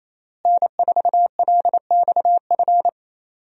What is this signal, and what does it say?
Morse code, heard as a keyed tone, about 27 wpm, 700 Hz. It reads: N4LXF
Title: USA Call Signs - 5 Characters - Repeated 2x - 28wpm